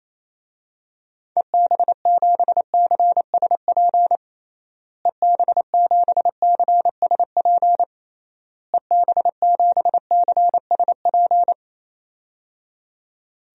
Morse code, heard as a keyed tone, about 28 wpm, 700 Hz. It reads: EB7CSP EB7CSP EB7CSP
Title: DXCC-100 Call Signs - 6 Characters - Repeated 3x - 28wpm